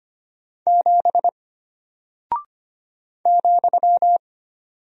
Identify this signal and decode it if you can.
Morse code, heard as a keyed tone, about 25 wpm, 700 Hz. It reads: 7 ,
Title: Single Character (Letter:Number:Punctuation:Prosign) Alt 25wpm